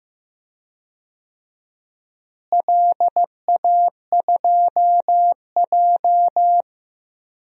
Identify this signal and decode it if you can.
Morse code, heard as a keyed tone, about 15 wpm, 700 Hz. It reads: LA2J
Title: DXCC-100 Call Signs - 4 Characters - Repeated 2x - 15wpm